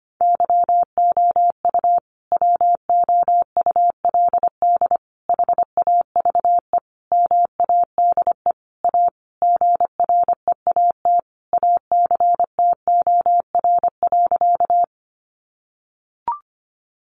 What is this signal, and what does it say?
Morse code, heard as a keyed tone, about 25 wpm, 700 Hz. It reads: YOU WOULD HAVE MADE A GREAT ACTOR.
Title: Sentences from Top 1500 Words 25wpm